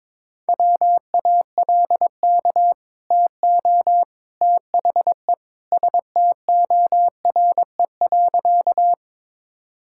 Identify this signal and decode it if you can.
Morse code, heard as a keyed tone, about 22 wpm, 700 Hz. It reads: WALK TO THE STORE.
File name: practice / 22wpm / Easy Sentences with Hints - 22wpm